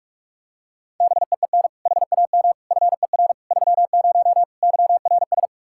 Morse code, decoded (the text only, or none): BEEN HAM FER 30 YRS